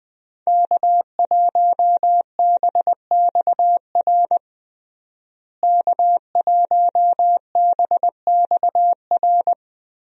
Morse code, heard as a keyed tone, about 20 wpm, 700 Hz. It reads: K1BXR K1BXR
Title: USA Call Signs - 5 Characters - Repeated 3x - 20wpm